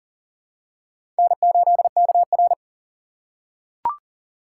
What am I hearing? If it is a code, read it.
Morse code, heard as a keyed tone, about 40 wpm, 700 Hz. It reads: N8KR
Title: USA Call Signs - 4 Characters - 4x Character Spacing - 40wpm